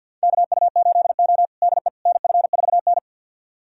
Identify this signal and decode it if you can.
Morse code, heard as a keyed tone, about 52 wpm, 700 Hz. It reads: KU8O DE NF4N
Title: USA Callsign DE Callsign - 4 Characters - 50wpm